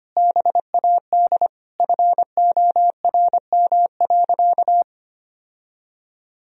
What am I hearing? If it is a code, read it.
Morse code, heard as a keyed tone, about 25 wpm, 700 Hz. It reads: BAD FORM.
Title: Sentences from Top 1000 Words - Review Part 1 - 25wpm